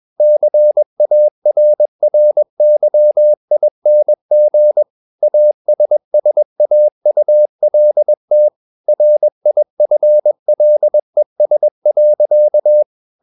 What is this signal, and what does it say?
Morse code, heard as a keyed tone, about 21 wpm, 600 Hz. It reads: CARRYING ASSAULT RIFLES.
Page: CW med Gnister